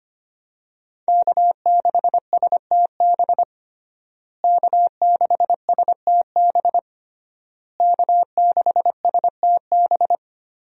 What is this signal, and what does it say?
Morse code, heard as a keyed tone, about 25 wpm, 700 Hz. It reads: K6STB K6STB K6STB
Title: USA Call Signs - 5 Characters - Repeated 3x - 25wpm